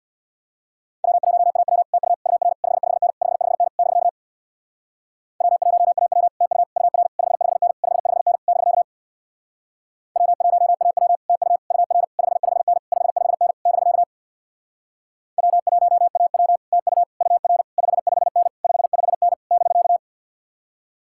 Morse code, heard as a keyed tone, about 67 wpm, 700 Hz. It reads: W1AW TU UR 55N 55N <BK> W1AW TU UR 55N 55N <BK> W1AW TU UR 55N 55N <BK> W1AW TU UR 55N 55N <BK>